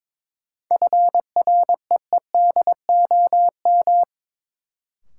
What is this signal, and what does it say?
Morse code, heard as a keyed tone, about 22 wpm, 700 Hz. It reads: FREEDOM
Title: Sentences from Top 800 Words - New Word Drill - 22wpm